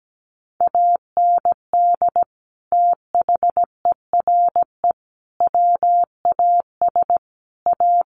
Morse code, heard as a keyed tone, about 17 wpm, 700 Hz. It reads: AND THERE WAS A
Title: Sets of 4 Words 17wpm